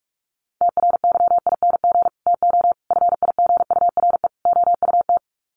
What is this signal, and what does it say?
Morse code, heard as a keyed tone, about 44 wpm, 700 Hz. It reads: TRYING TO FIGURE OUT
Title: Sets of 4 Words 45wpm